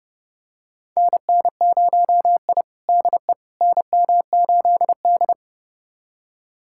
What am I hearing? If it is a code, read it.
Morse code, heard as a keyed tone, about 30 wpm, 700 Hz. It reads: NN0I DE NM8D
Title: USA Callsign DE Callsign - 4 Characters - 30wpm